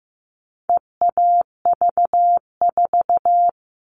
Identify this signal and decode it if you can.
Morse code, heard as a keyed tone, about 15 wpm, 700 Hz. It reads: EAV4
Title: All Cut Numbers - 4-Digit 15wpm